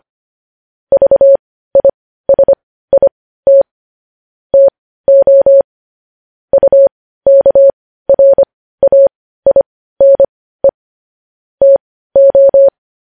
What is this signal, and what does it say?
Morse code, heard as a keyed tone, about 25 wpm, spaced as for 15 wpm, 550 Hz. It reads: VISIT TO UKRAINE TO
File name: podcast / News headlines-1x-25x15